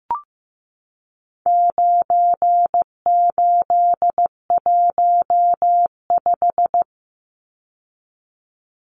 Morse code, heard as a keyed tone, about 15 wpm, 700 Hz. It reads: 9815
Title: Numbers - 4-Digit 15wpm